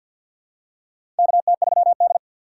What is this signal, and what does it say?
Morse code, heard as a keyed tone, about 47 wpm, 690 Hz. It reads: KT3D
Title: USA Call Signs - 6x Character Spacing - 50wpm